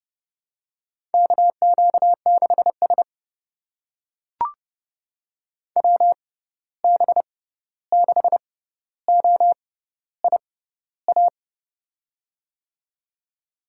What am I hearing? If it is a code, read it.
Morse code, heard as a keyed tone, about 30 wpm, spaced as for 14 wpm, 700 Hz. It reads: KQ6S WB6OIA
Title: USA Call Signs - 6x Character Spacing - 30wpm